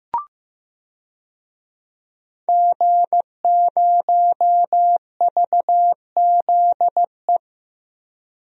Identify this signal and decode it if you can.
Morse code, heard as a keyed tone, about 15 wpm, 700 Hz. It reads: G0VZE
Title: DXCC-100 Call Signs - 5 Characters - Repeated 3x - 15wpm